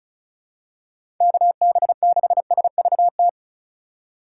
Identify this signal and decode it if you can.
Morse code, heard as a keyed tone, about 34 wpm, 700 Hz. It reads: KB6SVT
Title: USA Call Signs - 6 Characters - 2x Character Spacing - 35wpm